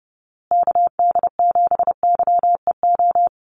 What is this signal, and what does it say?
Morse code, heard as a keyed tone, about 30 wpm, 700 Hz. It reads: KD7YEO
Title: USA Call Signs Repeated 2x 30wpm